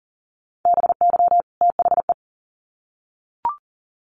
Morse code, heard as a keyed tone, about 40 wpm, 700 Hz. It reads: BY THE
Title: Sets of 2 Words 40wpm